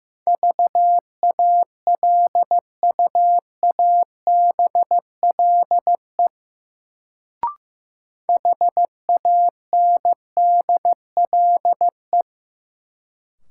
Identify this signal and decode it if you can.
Morse code, heard as a keyed tone, about 15 wpm, 700 Hz. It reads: VALUABLE HANDLE
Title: Sentences from Top 1100 Words - New Word Drill - 15wpm